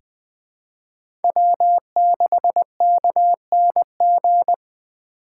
Morse code, heard as a keyed tone, about 21 wpm, 700 Hz. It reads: W6KNG
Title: USA Call Signs - 5 Characters - 2x Character Spacing - 20wpm